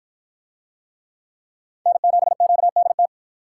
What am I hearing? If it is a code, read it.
Morse code, heard as a keyed tone, about 53 wpm, 700 Hz. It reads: N7QDT